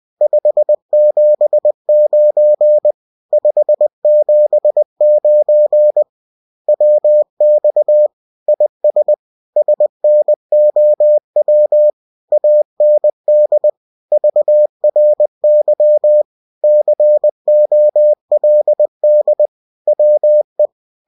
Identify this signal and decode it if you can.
Morse code, heard as a keyed tone, about 20 wpm, 600 Hz. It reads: 579 579 WX IS SNOW AND VRY COLD WE